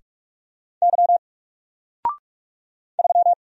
Morse code, heard as a keyed tone, about 44 wpm, 700 Hz. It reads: Y 3
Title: Single Character (Letter:Number:Punctuation:Prosign) Alt 45wpm